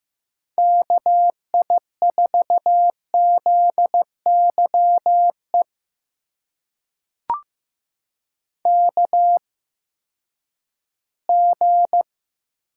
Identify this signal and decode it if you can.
Morse code, heard as a keyed tone, about 15 wpm, spaced as for 10 wpm, 700 Hz. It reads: KI4ZYE KG
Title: USA Call Signs - 6 Characters - 8x Character Spacing - 15wpm